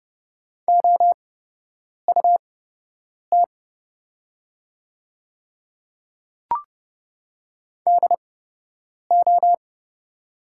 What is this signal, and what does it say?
Morse code, heard as a keyed tone, about 30 wpm, spaced as for 8 wpm, 700 Hz. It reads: OUT DO
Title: 8x Character Spacing - Top 100 Words - 30wpm